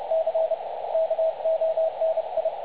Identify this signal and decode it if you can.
Morse code, heard as a keyed tone, about 31 wpm, 660 Hz. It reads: C KONI
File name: dlazaawansowanych / nagraniazpasma / websdr recording 3565.0kHz